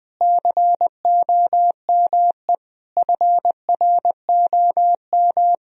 Morse code, heard as a keyed tone, about 20 wpm, 700 Hz. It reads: COME FROM
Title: Sentences from Top 1000 Words - Review Part 4 - 20wpm